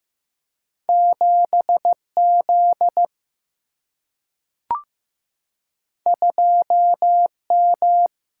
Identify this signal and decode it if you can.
Morse code, heard as a keyed tone, about 15 wpm, 700 Hz. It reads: 7Z 2M
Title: Single-Letter Single-Number 15wpm